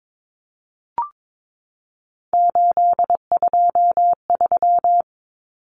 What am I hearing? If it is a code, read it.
Morse code, heard as a keyed tone, about 22 wpm, 700 Hz. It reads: 823
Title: Common Cut Numbers - 3-Digit 22wpm